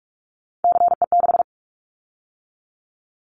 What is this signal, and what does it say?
Morse code, heard as a keyed tone, about 44 wpm, 700 Hz. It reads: CE6
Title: Callsign Prefixes for Top 100 DXCC - 45wpm